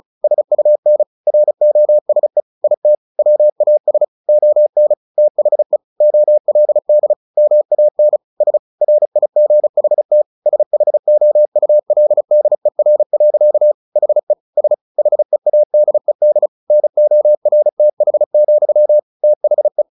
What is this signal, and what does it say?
Morse code, heard as a keyed tone, about 34 wpm, 600 Hz. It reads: SUN ROSE IT WAS ON THE OLD MAN S RIGHT SHOULDER. HE S HEADED NORTH, THE
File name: the old man and the sea / the old man and the sea 1 35wpm